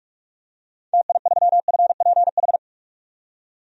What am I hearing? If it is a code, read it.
Morse code, heard as a keyed tone, about 42 wpm, 700 Hz. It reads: TI3FPH
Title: DXCC-100 Call Signs - 6 Characters - 45wpm